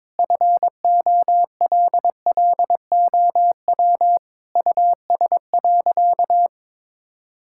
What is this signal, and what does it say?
Morse code, heard as a keyed tone, about 21 wpm, 700 Hz. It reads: FOLLOW US.
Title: Sentences from Top 1000 Words - Review Part 6 - 22wpm